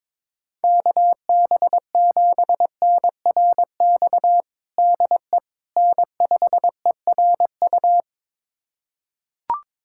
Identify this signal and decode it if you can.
Morse code, heard as a keyed tone, about 22 wpm, 700 Hz. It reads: KB7NRX DE N5ERU
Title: USA Callsign DE Callsign 22wpm